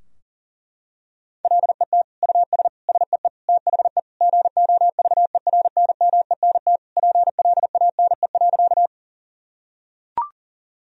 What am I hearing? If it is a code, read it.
Morse code, heard as a keyed tone, about 40 wpm, 700 Hz. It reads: LET US SEE THE GOVERNMENT PLANE.
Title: Sentences from Top 1000 Words - Review Part 3 - 40wpm